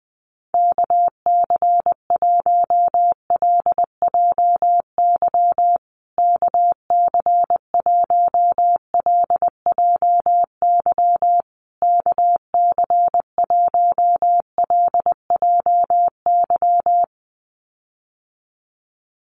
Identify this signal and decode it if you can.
Morse code, heard as a keyed tone, about 19 wpm, 700 Hz. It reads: KC1LJY KC1LJY KC1LJY
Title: USA Call Signs Repeated 3x 20wpm